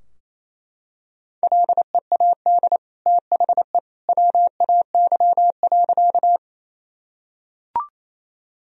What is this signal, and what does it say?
Morse code, heard as a keyed tone, about 27 wpm, 700 Hz. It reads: LEAD THE WAY.
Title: Sentences from Top 1000 Words - Review Part 2 - 28wpm